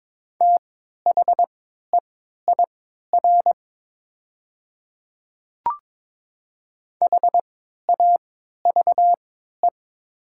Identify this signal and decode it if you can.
Morse code, heard as a keyed tone, about 22 wpm, spaced as for 13 wpm, 700 Hz. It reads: THEIR HAVE
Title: 3x Character Spacing - Top 100 Words - 22wpm